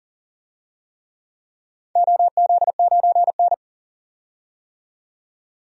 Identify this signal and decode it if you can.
Morse code, heard as a keyed tone, about 40 wpm, 700 Hz. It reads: OZ9N
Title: DXCC-100 Call Signs - 4 Characters - 40wpm